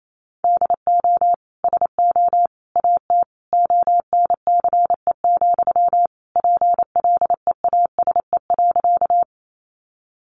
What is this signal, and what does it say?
Morse code, heard as a keyed tone, about 28 wpm, 700 Hz. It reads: DO SO AT ONCE, PLEASE.
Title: Sentences from Top 1500 Words 28wpm